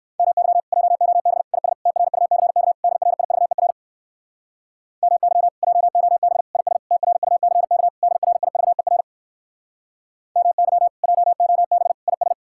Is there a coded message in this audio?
Yes: Morse code from 0.2 to 12.3 s, heard as a keyed tone, about 62 wpm, 700 Hz. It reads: MY JOB IS TRUCK DRIVER MY JOB IS TRUCK DRIVER MY JOB IS